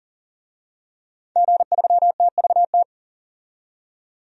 Morse code, heard as a keyed tone, about 40 wpm, 700 Hz. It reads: G3TVT
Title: DXCC-100 Call Signs - 5 Characters - 40wpm